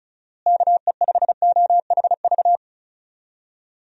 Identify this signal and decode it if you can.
Morse code, heard as a keyed tone, about 36 wpm, 700 Hz. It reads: KE5OHV
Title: USA Call Signs - 6 Characters - Repeated 3x - 35wpm